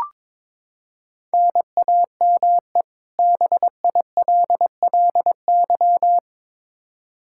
Morse code, heard as a keyed tone, about 21 wpm, 700 Hz. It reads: NAME BILLY
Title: Practice QSO - Exchange 1 - Beginning - 22wpm